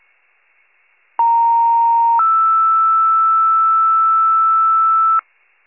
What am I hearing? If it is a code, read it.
Two-tone page, heard as a one-second tone, then a three-second tone, about 930 then 1320 Hz, with noise. It is heard from 1.2 to 5.2 s.